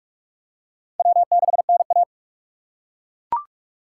Morse code, heard as a keyed tone, about 42 wpm, 700 Hz. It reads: W6NA
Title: USA Call Signs - 2x Character Spacing - 45wpm